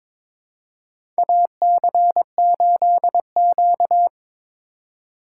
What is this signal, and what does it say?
Morse code, heard as a keyed tone, about 21 wpm, 700 Hz. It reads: AC8Q
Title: USA Call Signs - 4 Characters - 2x Character Spacing - 22wpm